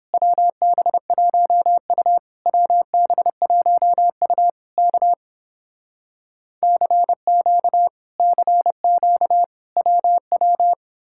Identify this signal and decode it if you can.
Morse code, heard as a keyed tone, about 29 wpm, 700 Hz. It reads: WB1U WB1U K CQ CQ WW